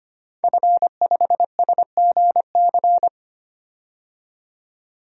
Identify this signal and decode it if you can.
Morse code, heard as a keyed tone, about 25 wpm, 700 Hz. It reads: F5SGC
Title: DXCC-100 Call Signs - 5 Characters - 25wpm